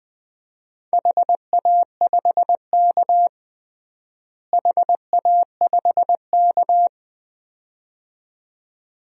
Morse code, heard as a keyed tone, about 20 wpm, 700 Hz. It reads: HA5K HA5K
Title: DXCC-100 Call Signs - 4 Characters - Repeated 2x - 20wpm